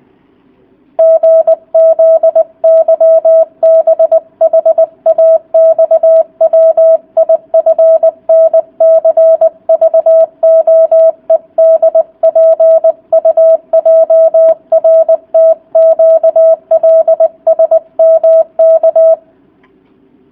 Morse code, heard as a keyed tone, about 20 wpm, 650 Hz. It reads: GZYBHAXWIFNCVOEDPUJRTQLSMK